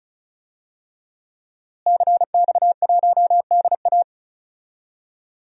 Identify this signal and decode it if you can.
Morse code, heard as a keyed tone, about 36 wpm, 700 Hz. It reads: CX1DA
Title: DXCC-100 Call Signs - 5 Characters - Repeated 3x - 35wpm